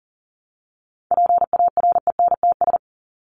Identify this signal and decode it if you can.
Morse code, heard as a keyed tone, about 40 wpm, 700 Hz. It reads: PARENTS
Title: ES:S Suffix - Top 100 Words with Base Words 40wpm